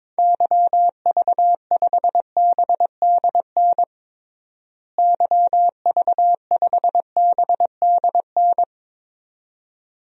Morse code, heard as a keyed tone, about 22 wpm, 700 Hz. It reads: YV5BDN YV5BDN
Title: DXCC-100 Call Signs - 6 Characters - Repeated 2x - 22wpm